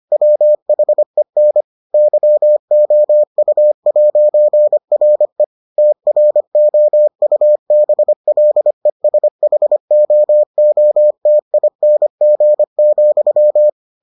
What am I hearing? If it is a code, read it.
Morse code, heard as a keyed tone, about 25 wpm, 600 Hz. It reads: WHEN YOU'RE TROUBLESHOOTING,